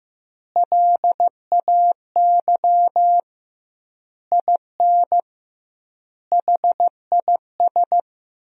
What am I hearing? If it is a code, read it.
Morse code, heard as a keyed tone, about 15 wpm, 700 Hz. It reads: LAY IN HIS
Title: 2x Word-Spacing - Sentences from Top 500 Words - 15wpm